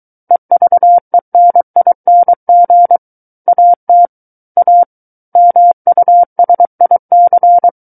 Morse code, heard as a keyed tone, about 23 wpm, 700 Hz. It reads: EVENING AT A MUSIC
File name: Book One / 14-Chapter 14 - 23 WPM